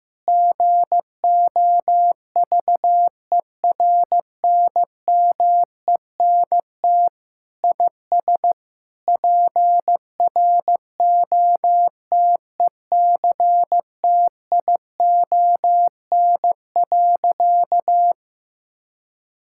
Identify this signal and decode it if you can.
Morse code, heard as a keyed tone, about 16 wpm, 700 Hz. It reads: GOVERNMENT IS PROTECTION.
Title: Sentences from Top 1600 Words 15wpm